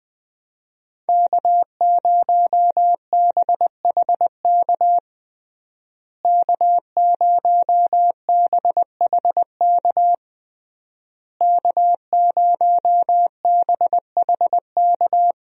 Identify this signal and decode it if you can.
Morse code, heard as a keyed tone, about 20 wpm, 700 Hz. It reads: K0BHK K0BHK K0BHK
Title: USA Call Signs - 5 Characters - Repeated 3x - 20wpm